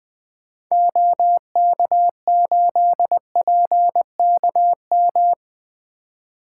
Morse code, heard as a keyed tone, about 20 wpm, 700 Hz. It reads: OK8PKM
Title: 2023 SOTA Activator Call Signs - 20wpm